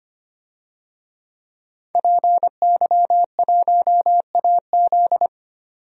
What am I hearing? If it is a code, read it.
Morse code, heard as a keyed tone, about 25 wpm, 700 Hz. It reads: PY1AZ
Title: DXCC-100 Call Signs - 5 Characters - 25wpm